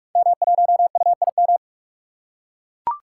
Morse code, heard as a keyed tone, about 44 wpm, 700 Hz. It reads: M1UIM
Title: Code Groups - 5 Letter:Numbers - 45wpm